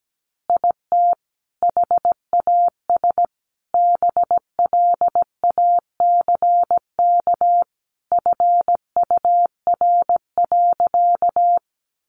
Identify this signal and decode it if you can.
Morse code, heard as a keyed tone, about 17 wpm, 700 Hz. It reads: IT HAS BLACK FUR.
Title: Easy Phrases 17wpm